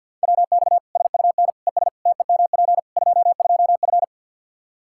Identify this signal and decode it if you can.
Morse code, heard as a keyed tone, about 52 wpm, 700 Hz. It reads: WX SUN ES TEMP 22F
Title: Practice QSO - Exchange 2 - Info - 50wpm